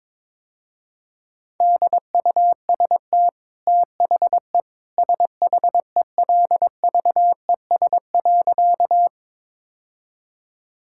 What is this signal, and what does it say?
Morse code, heard as a keyed tone, about 22 wpm, 700 Hz. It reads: DUST THE SHELVES.
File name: practice / 22wpm / Easy Sentences with Hints - 22wpm